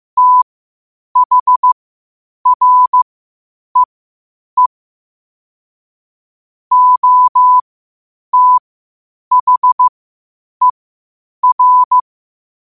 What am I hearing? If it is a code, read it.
Morse code, heard as a keyed tone, about 15 wpm, spaced as for 8 wpm, 1000 Hz. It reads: THREE OTHER
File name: 8wpm